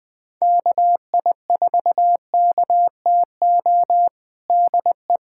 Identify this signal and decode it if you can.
Morse code, heard as a keyed tone, about 19 wpm, 700 Hz. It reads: KI4KTO DE
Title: USA Callsign DE Callsign - 6 Characters - 20wpm